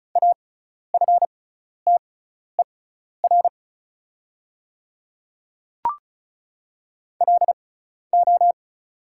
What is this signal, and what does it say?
Morse code, heard as a keyed tone, about 36 wpm, spaced as for 12 wpm, 690 Hz. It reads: AFTER LO
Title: 6x Character Spacing - Top 300 Words - 35wpm